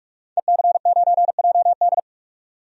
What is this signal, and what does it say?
Morse code, heard as a keyed tone, about 47 wpm, 700 Hz. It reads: EC9JD
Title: DXCC-100 Call Signs - 5 Characters - 45wpm